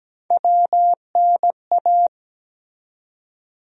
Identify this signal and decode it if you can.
Morse code, heard as a keyed tone, about 17 wpm, 700 Hz. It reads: WNA
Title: USA Call Sign Postfixes - 3 Letters 17wpm